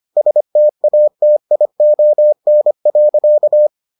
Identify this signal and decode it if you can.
Morse code, heard as a keyed tone, about 25 wpm, 600 Hz. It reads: STATION.